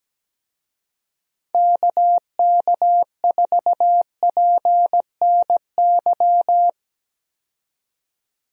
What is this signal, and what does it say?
Morse code, heard as a keyed tone, about 17 wpm, 700 Hz. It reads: KK4PNY
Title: USA Call Signs - 6 Characters - 17wpm